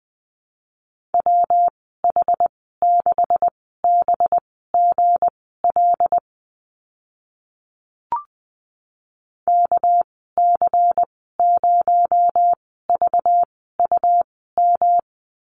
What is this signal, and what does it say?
Morse code, heard as a keyed tone, about 20 wpm, spaced as for 14 wpm, 700 Hz. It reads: WH6BGL KC0VUM
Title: 219 Lesson 35g - Number 0 - Call Signs 20wpm